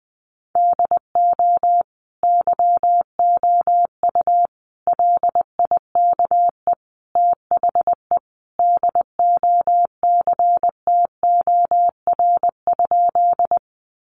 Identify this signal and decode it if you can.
Morse code, heard as a keyed tone, about 20 wpm, 700 Hz. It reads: DO YOU LIKE THE DOCTOR?
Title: Sentences from Top 1500 Words 20wpm